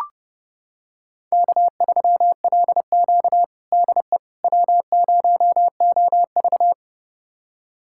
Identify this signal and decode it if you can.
Morse code, heard as a keyed tone, about 32 wpm, 700 Hz. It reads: K3LQ DE W0OV
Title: USA Callsign DE Callsign - 4 Characters - 30wpm